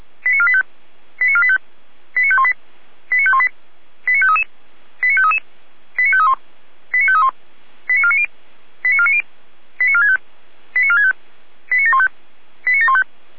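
Five-tone selective call (ZVEI), heard as quick runs of five tones, tones of about 70 ms. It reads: 89485 89485 89518 89518 89533 89533 89531 89531 89490 89490 89465 89465 89715 89715
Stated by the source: Five-tone ZVEI selective call received by SDR.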